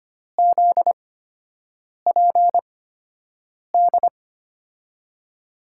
Morse code, heard as a keyed tone, about 25 wpm, spaced as for 7 wpm, 700 Hz. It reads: ZPD
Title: USA Call Signs - 6 Characters - 8x Character Spacing - 25wpm